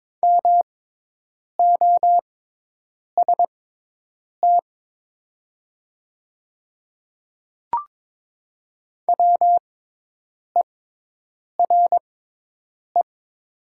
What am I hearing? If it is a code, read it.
Morse code, heard as a keyed tone, about 22 wpm, spaced as for 8 wpm, 700 Hz. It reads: MOST WERE
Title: 6x Character Spacing - Top 100 Words - 22wpm